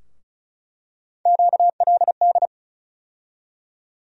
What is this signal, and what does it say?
Morse code, heard as a keyed tone, about 36 wpm, 700 Hz. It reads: QLD
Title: USA Call Sign Postfixes - 3 Letters 35wpm